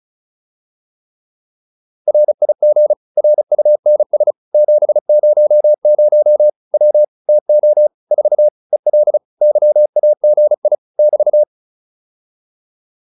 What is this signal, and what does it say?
Morse code, heard as a keyed tone, about 34 wpm, 600 Hz. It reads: RIG RUNS 700 W TO 4 EL YAGI <BT>